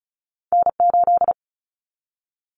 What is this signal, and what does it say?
Morse code, heard as a keyed tone, about 36 wpm, 700 Hz. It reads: N8
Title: Common Cut Numbers - 2-Digit 35wpm